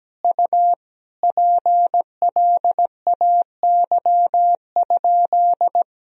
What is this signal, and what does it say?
Morse code, heard as a keyed tone, about 17 wpm, 700 Hz. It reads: U PLAY?
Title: Sentences from Top 200 Words and Abbreviations 17wpm